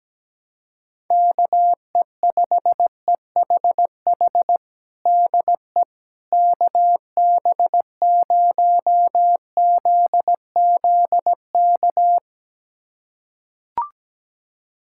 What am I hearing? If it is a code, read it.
Morse code, heard as a keyed tone, about 17 wpm, 700 Hz. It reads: KE5EHH DE KB0ZZK
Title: USA Callsign DE Callsign - 6 Characters - 17wpm